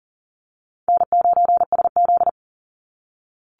Morse code, heard as a keyed tone, about 37 wpm, 700 Hz. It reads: N9SZ
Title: Easy Call Signs - 40wpm